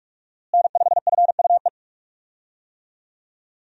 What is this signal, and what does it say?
Morse code, heard as a keyed tone, about 41 wpm, 700 Hz. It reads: N5FUE
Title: USA Call Signs - 5 Characters - Repeated 2x - 45wpm